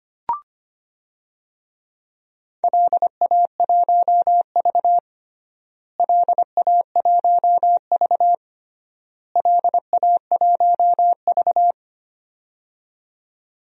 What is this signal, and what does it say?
Morse code, heard as a keyed tone, about 25 wpm, 700 Hz. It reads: LA1V LA1V LA1V
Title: DXCC-100 Call Signs - 4 Characters - Repeated 3x - 25wpm